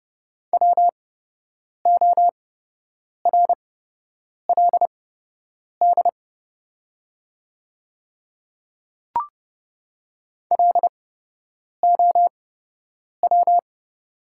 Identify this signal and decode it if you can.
Morse code, heard as a keyed tone, about 30 wpm, spaced as for 8 wpm, 690 Hz. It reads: WORLD LOW
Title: 8x Character Spacing - Top 300 Words - 30wpm